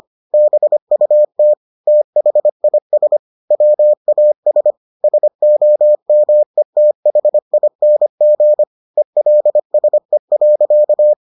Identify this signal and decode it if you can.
Morse code, heard as a keyed tone, about 25 wpm, 600 Hz. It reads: BUT THIS WAS SOMETHING ELSE.